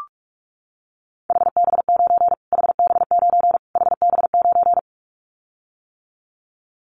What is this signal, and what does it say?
Morse code, heard as a keyed tone, about 44 wpm, 700 Hz. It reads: HB9 HB9 HB9
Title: Callsign Prefixes for Top 25 DXCC - Repeated 3x - 45wpm